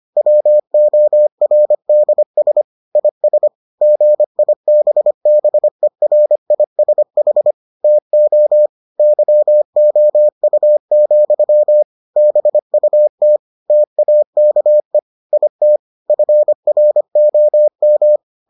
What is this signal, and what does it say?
Morse code, heard as a keyed tone, about 25 wpm, 600 Hz. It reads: WORDS IS GIBBERISH TO YOU, BUT TAKE IT FROM